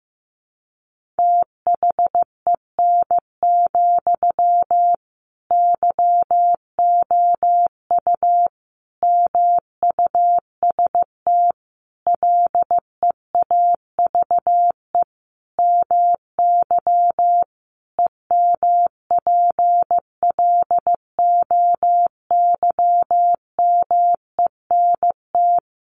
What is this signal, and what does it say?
Morse code, heard as a keyed tone, about 15 wpm, 700 Hz. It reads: THEN, YOU MUST LEAVE MY EMPLOYMENT
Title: Sentences from Top 1500 Words 15wpm